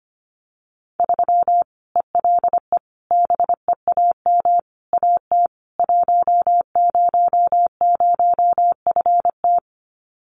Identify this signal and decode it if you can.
Morse code, heard as a keyed tone, about 25 wpm, 700 Hz. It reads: 3 ELE BEAM AT 100FT
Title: Common QSO Phrases 25wpm